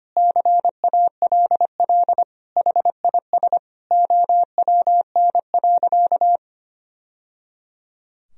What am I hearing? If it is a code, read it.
Morse code, heard as a keyed tone, about 25 wpm, 700 Hz. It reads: CALL HIS OWN.
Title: Sentences from Top 1000 Words - Review Part 2 - 25wpm